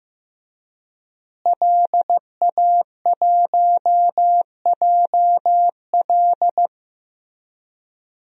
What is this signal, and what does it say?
Morse code, heard as a keyed tone, about 15 wpm, 700 Hz. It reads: LA1JL
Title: DXCC-100 Call Signs - 5 Characters - Repeated 2x - 15wpm